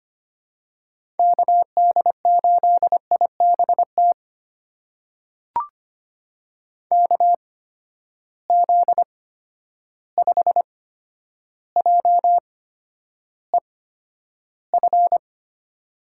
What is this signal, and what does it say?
Morse code, heard as a keyed tone, about 25 wpm, spaced as for 11 wpm, 700 Hz. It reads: KD8IBT KZ5JEF
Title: USA Call Signs - 6 Characters - 8x Character Spacing - 25wpm